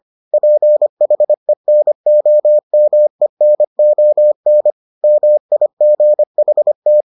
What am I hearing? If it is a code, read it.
Morse code, heard as a keyed tone, about 25 wpm, 600 Hz. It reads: PHENOMENON MIGHT